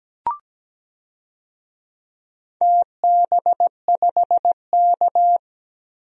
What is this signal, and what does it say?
Morse code, heard as a keyed tone, about 17 wpm, 700 Hz. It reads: TB5K
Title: DXCC-100 Call Signs - 4 Characters - 17wpm